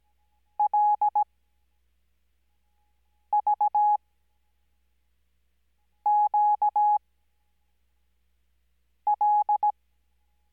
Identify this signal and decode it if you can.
Morse code, heard as a keyed tone, about 17 wpm, spaced as for 4 wpm, 840 Hz. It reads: LVQL
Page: Farnsworth Spacing: 15WPM Character speed Spaced at 5WPM.